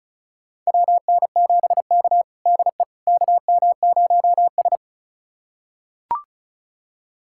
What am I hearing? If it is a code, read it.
Morse code, heard as a keyed tone, about 36 wpm, 700 Hz. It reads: WN7K DE KM0S
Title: USA Callsign DE Callsign - 4 Characters - 35wpm